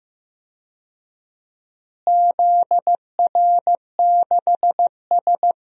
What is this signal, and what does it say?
Morse code, heard as a keyed tone, about 15 wpm, 700 Hz. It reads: ZR6S
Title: DXCC-100 Call Signs - 4 Characters - Repeated 3x - 15wpm